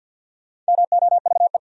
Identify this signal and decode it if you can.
Morse code, heard as a keyed tone, about 66 wpm, spaced as for 58 wpm, 700 Hz. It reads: MOVE
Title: Speed-Racing Top 200 Words 50wpm